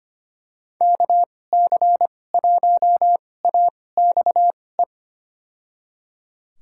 Morse code, heard as a keyed tone, about 25 wpm, spaced as for 18 wpm, 700 Hz. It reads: KC1AXE
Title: USA Call Signs - 6 Characters - 2x Character Spacing - 25wpm